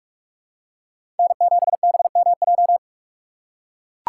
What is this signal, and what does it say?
Morse code, heard as a keyed tone, about 43 wpm, 700 Hz. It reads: N7BMJ
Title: USA Call Signs - 5 Characters - 2x Character Spacing - 45wpm